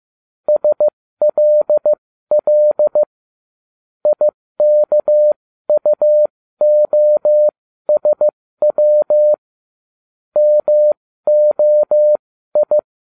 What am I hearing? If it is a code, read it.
Morse code, heard as a keyed tone, about 15 wpm, spaced as for 13 wpm, 600 Hz. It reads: SLL IKUOSW MOI